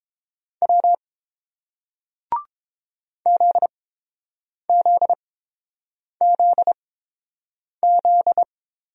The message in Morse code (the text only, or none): W ZZZZ